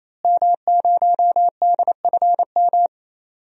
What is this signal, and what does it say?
Morse code, heard as a keyed tone, about 27 wpm, 700 Hz. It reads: M0DFM
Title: DXCC-100 Call Signs - 5 Characters - Repeated 3x - 28wpm